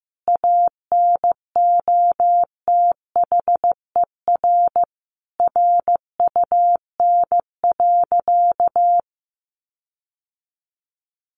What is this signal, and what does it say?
Morse code, heard as a keyed tone, about 15 wpm, 700 Hz. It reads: ANOTHER RUN.
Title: Sentences from Top 1500 Words 15wpm